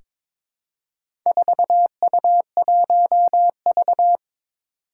Morse code, heard as a keyed tone, about 22 wpm, 700 Hz. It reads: 4U1V
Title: Four Challenging Letter-Number Combinations 22wpm